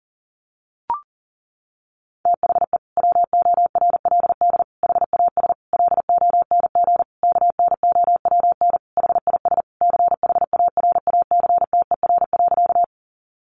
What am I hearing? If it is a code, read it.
Morse code, heard as a keyed tone, about 40 wpm, 700 Hz. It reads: THE WORLD HAS LONG KNOWN HIS CHARACTER.
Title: Sentences from Top 1200 Words 40wpm